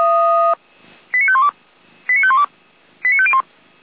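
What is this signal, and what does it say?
Five-tone selective call (ZVEI), heard as quick runs of five tones, tones of about 70 ms. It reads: 89512 89512 89501
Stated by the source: ZVEI five-tone selective call received by SDR.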